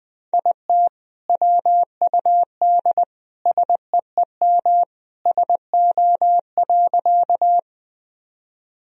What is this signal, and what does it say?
Morse code, heard as a keyed tone, about 20 wpm, 700 Hz. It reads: IT WUD SEEM SO.
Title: Sentences from Top 300 Words and Abbreviations 20wpm